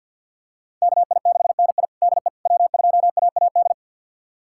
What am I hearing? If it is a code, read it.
Morse code, heard as a keyed tone, about 51 wpm, 700 Hz. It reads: KI6NI DE W2AAD
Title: USA Callsign DE Callsign - 5 Characters - 50wpm